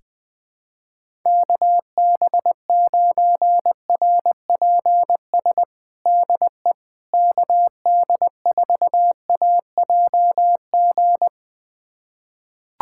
Morse code, heard as a keyed tone, about 20 wpm, 700 Hz. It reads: KB9RPS DE KD4AJG
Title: USA Callsign DE Callsign - 6 Characters - 20wpm